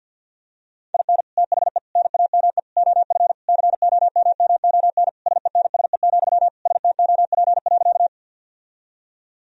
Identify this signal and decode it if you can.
Morse code, heard as a keyed tone, about 50 wpm, 700 Hz. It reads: IN THE NAME OF COMMON SENSE, STOP.
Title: Sentences from Top 1000 Words - Review Part 4 - 50wpm